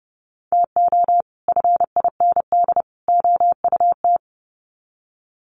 Morse code, heard as a keyed tone, about 30 wpm, 700 Hz. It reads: TO FIND OUT
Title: Sets of 3 Words 30wpm